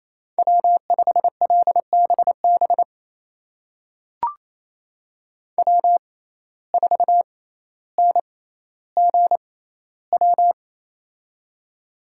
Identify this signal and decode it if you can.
Morse code, heard as a keyed tone, about 28 wpm, spaced as for 14 wpm, 700 Hz. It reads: W5LBB W4NGW
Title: USA Call Signs - 5 Characters - 6x Character Spacing - 28wpm